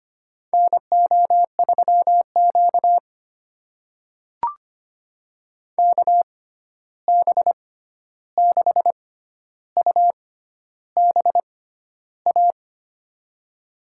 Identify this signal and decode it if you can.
Morse code, heard as a keyed tone, about 25 wpm, spaced as for 11 wpm, 700 Hz. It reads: NO3Q KB6UBA
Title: USA Call Signs - 6x Character Spacing - 25wpm